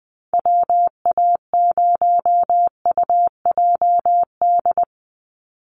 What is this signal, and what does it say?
Morse code, heard as a keyed tone, about 19 wpm, 700 Hz. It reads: WA0UJD
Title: USA Call Signs 20wpm